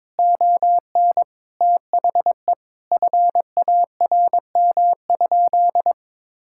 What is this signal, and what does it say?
Morse code, heard as a keyed tone, about 22 wpm, 700 Hz. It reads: ON THE FARM?
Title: Sentences from Top 1000 Words - Review Part 5 - 22wpm